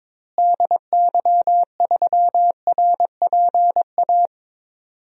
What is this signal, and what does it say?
Morse code, heard as a keyed tone, about 22 wpm, 700 Hz. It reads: DY3RPA
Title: DXCC-100 Call Signs - 6 Characters - Repeated 2x - 22wpm